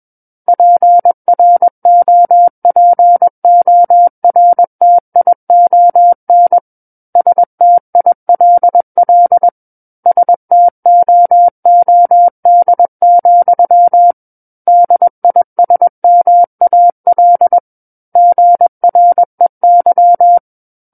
Morse code, heard as a keyed tone, about 21 wpm, 700 Hz. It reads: PROPORTION STILL STOOD, DISMAL GREY
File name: Book One / 12-Chapter 12 - 21 WPM